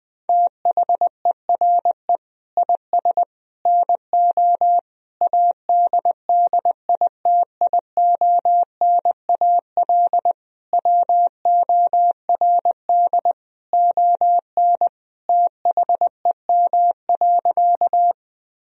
Morse code, heard as a keyed tone, about 20 wpm, 700 Hz. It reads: THERE IS NO ADDITIONAL WORD ON THEM.
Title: Sentences from Top 1000 Words - Review Part 4 - 20wpm